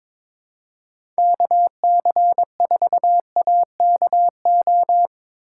Morse code, heard as a keyed tone, about 22 wpm, 700 Hz. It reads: KC4AKO
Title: USA Call Signs - 6 Characters - 2x Character Spacing - 22wpm